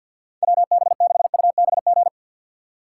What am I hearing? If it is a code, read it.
Morse code, heard as a keyed tone, about 53 wpm, 700 Hz. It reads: WB6UBG
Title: USA Call Signs - 6 Characters - 6x Character Spacing - 50wpm